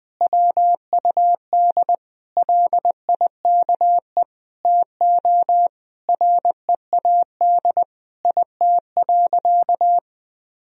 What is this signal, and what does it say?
Morse code, heard as a keyed tone, about 20 wpm, 700 Hz. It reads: WUD LIKE TO READ IT.
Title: Sentences from Top 500 Words and Abbreviations 20wpm